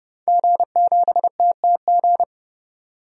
Morse code, heard as a keyed tone, about 30 wpm, 700 Hz. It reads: G7TTG
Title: DXCC-100 Call Signs - 5 Characters - 30wpm